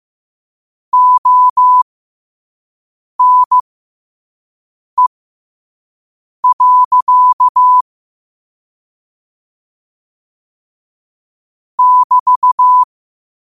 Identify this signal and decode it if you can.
Morse code, heard as a keyed tone, about 15 wpm, spaced as for 5 wpm, 1000 Hz. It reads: ONE. <BT>
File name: qotdfiles / 5wpm